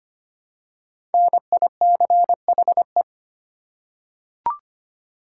Morse code, heard as a keyed tone, about 25 wpm, 700 Hz. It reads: NICHE
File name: 5 Letter Words - First 3 Letters Given - 25wpm